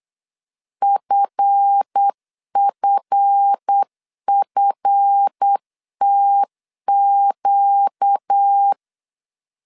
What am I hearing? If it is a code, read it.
Morse code, heard as a keyed tone, about 8 wpm, 800 Hz. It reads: FFFTQ
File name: LEZIONI / 7 GF9Z / 7 GF 8wpm 3108